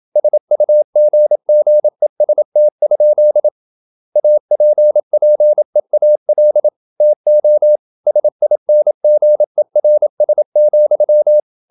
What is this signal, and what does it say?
Morse code, heard as a keyed tone, about 28 wpm, 600 Hz. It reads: SUGGEST? APPEAL TO SINGERS,
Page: Never 26wpm | CW med Gnister